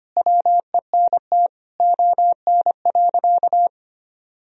Morse code, heard as a keyed tone, about 25 wpm, 700 Hz. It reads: WENT ON.
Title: Sentences from Top 1000 Words - Review Part 1 - 25wpm